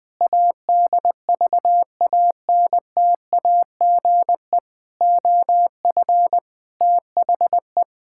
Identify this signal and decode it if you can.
Morse code, heard as a keyed tone, about 20 wpm, 700 Hz. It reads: ADVANTAGE OF THE
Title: Build A Phrase - 5-Word Phrases - 20wpm